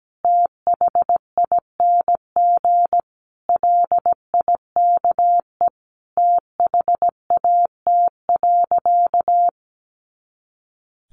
Morse code, heard as a keyed tone, about 17 wpm, 700 Hz. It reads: THING LIKE THAT.
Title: A Sound Of Thunder 17wpm